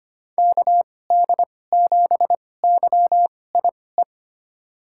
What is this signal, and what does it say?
Morse code, heard as a keyed tone, about 25 wpm, spaced as for 18 wpm, 700 Hz. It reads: KD7YIE
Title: USA Call Signs - 2x Character Spacing - 25wpm